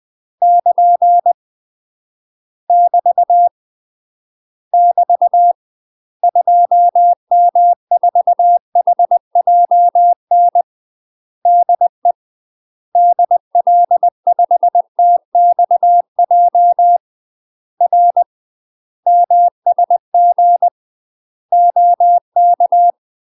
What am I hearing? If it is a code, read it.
Morse code, heard as a keyed tone, about 20 wpm, 700 Hz. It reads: <KN> <BT> <BT> 2M4HJN DE DL5TXJ R MSG OK
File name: UK MASTER QSOs TRAINING/20WPM